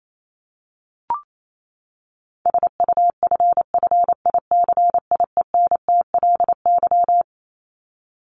Morse code, heard as a keyed tone, about 28 wpm, 700 Hz. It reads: SUFFICIENTLY
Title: LY Suffix - Top 100 Words with Base Words 28wpm